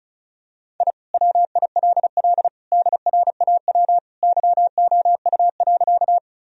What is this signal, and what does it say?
Morse code, heard as a keyed tone, about 36 wpm, 700 Hz. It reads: I WILL DRAW YOU.
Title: Sentences from Top 1000 Words - Review Part 1 - 35wpm